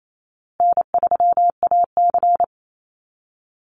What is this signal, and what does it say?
Morse code, heard as a keyed tone, about 28 wpm, 700 Hz. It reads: N3AC
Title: Easy Call Signs Repeated 3x - 28wpm